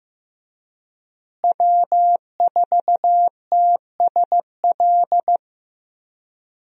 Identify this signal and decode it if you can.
Morse code, heard as a keyed tone, about 16 wpm, 700 Hz. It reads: W4TSL
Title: USA Call Signs - 5 Characters - Repeated 2x - 15wpm